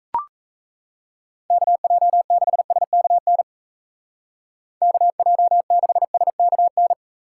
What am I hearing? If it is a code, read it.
Morse code, heard as a keyed tone, about 34 wpm, 700 Hz. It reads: KJ6SKN KJ6SKN
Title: Speed Racing - USA 2x3 Call Signs 28wpm